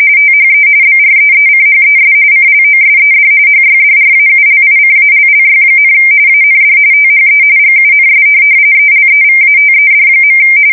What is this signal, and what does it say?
RTTY, about 45.45 baud. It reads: THE BREECHES CLOSED HYDRAULICALLY, AND THE GUNS ELEVATED.
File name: fsk170 / USS-New-Jersey